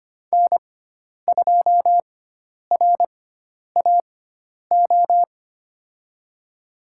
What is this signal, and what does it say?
Morse code, heard as a keyed tone, about 25 wpm, spaced as for 10 wpm, 700 Hz. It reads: N2RAO
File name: Lesson 16g - Letter c - Call Signs 25wpm